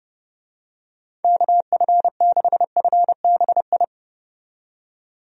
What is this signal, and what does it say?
Morse code, heard as a keyed tone, about 30 wpm, 700 Hz. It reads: KF6FBI
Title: USA Call Signs - 6 Characters - 30wpm